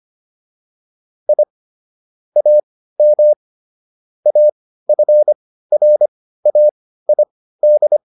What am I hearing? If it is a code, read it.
Morse code, heard as a keyed tone, about 25 wpm, spaced as for 15 wpm, 600 Hz. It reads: I AM AFRAID